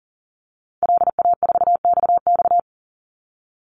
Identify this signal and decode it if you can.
Morse code, heard as a keyed tone, about 39 wpm, 700 Hz. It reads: LA4XX
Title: Call Signs - 40wpm